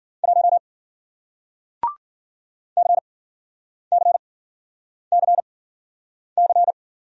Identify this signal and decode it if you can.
Morse code, heard as a keyed tone, about 55 wpm, spaced as for 9 wpm, 700 Hz. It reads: 1 CCCC